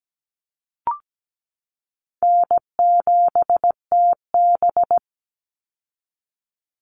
Morse code, heard as a keyed tone, about 17 wpm, 700 Hz. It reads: N7TB
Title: Call Signs Repeated 2x - 17wpm